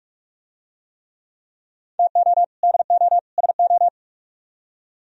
Morse code, heard as a keyed tone, about 44 wpm, 700 Hz. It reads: TO DO SO